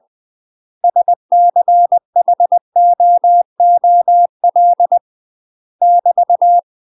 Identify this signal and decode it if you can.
Morse code, heard as a keyed tone, about 20 wpm, 700 Hz. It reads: SCHOOL <BT>
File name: UK MASTER QSOs TRAINING/20WPM